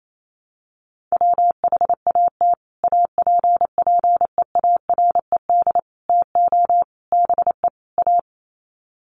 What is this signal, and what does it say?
Morse code, heard as a keyed tone, about 28 wpm, 700 Hz. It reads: WHAT APPEARED TO BE A
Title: Sets of 5 Words Encore2 28wpm